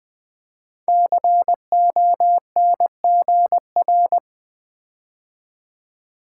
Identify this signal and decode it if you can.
Morse code, heard as a keyed tone, about 20 wpm, 700 Hz. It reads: CONGR
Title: Build A Word - 5 Letters At A Time - 12-Letter Words - 20wpm